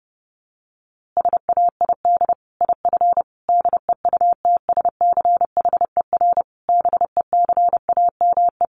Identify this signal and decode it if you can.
Morse code, heard as a keyed tone, about 30 wpm, 700 Hz. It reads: SAID IF DEUTSCHER BECAME
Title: A Sound Of Thunder 30wpm